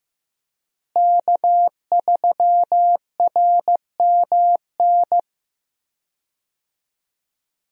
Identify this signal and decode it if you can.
Morse code, heard as a keyed tone, about 15 wpm, 700 Hz. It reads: K3RMN
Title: USA Call Signs - 5 Characters - Repeated 3x - 15wpm